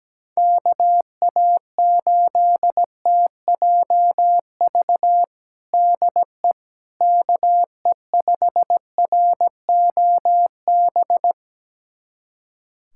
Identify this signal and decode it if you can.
Morse code, heard as a keyed tone, about 17 wpm, 700 Hz. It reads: KA8TJV DE KE5ROB
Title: USA Callsign DE Callsign 17wpm